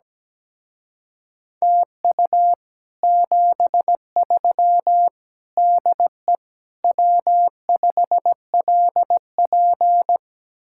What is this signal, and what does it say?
Morse code, heard as a keyed tone, about 17 wpm, 700 Hz. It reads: TU 73 DE W5LP
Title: Speed-Racing Common QSO Phrases Encore - 17wpm